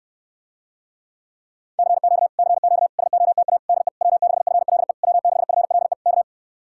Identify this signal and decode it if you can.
Morse code, heard as a keyed tone, about 69 wpm, 700 Hz. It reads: CQ CQ SOTA DE W6FLE W6FLE K